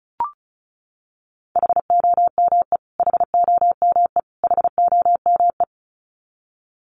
Morse code, heard as a keyed tone, about 34 wpm, 700 Hz. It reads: HOME HOME HOME
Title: Top 500 Words Repeated 35wpm